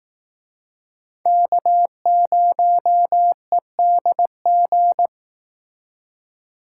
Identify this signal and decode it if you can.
Morse code, heard as a keyed tone, about 19 wpm, 700 Hz. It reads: K0EDG
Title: Speed Racing - USA 1x3 Call Signs 15wpm